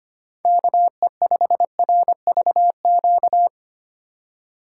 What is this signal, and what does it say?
Morse code, heard as a keyed tone, about 25 wpm, 700 Hz. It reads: KE5RVQ
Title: USA Call Signs - 6 Characters - 2x Character Spacing - 25wpm